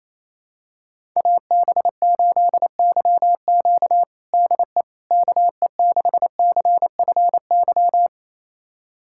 Morse code, heard as a keyed tone, about 28 wpm, 700 Hz. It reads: AB8YQ DE KE6CFY
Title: USA Callsign DE Callsign 28wpm